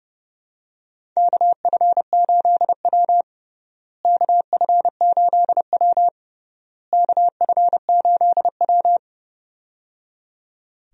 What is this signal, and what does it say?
Morse code, heard as a keyed tone, about 30 wpm, 700 Hz. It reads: KF8W KF8W KF8W
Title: USA Call Signs - 4 Characters - Repeated 3x - 30wpm